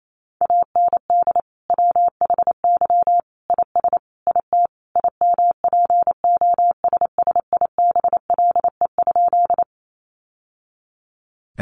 Morse code, heard as a keyed tone, about 28 wpm, 700 Hz. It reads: AND WHY IS IT IMPOSSIBLE?
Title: Sentences from Top 1100 Words 28wpm